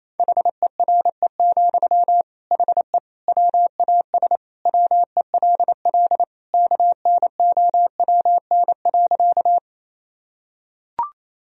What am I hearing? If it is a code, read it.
Morse code, heard as a keyed tone, about 28 wpm, 700 Hz. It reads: HERE, HE WAS WELL KNOWN.
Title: Sentences from Top 1000 Words - Review Part 3 - 28wpm